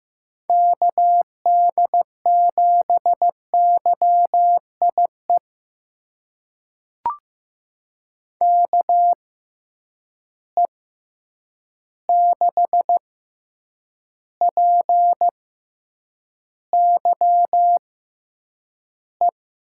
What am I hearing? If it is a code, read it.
Morse code, heard as a keyed tone, about 15 wpm, spaced as for 8 wpm, 700 Hz. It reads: KD7YIE KE6PYE
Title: USA Call Signs - 6 Characters - 6x Character Spacing - 15wpm